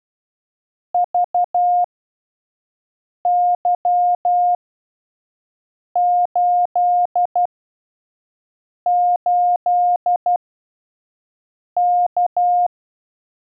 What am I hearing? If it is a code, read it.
Morse code, heard as a keyed tone, about 12 wpm, spaced as for 5 wpm, 710 Hz. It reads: VY88K
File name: 5WPM test 1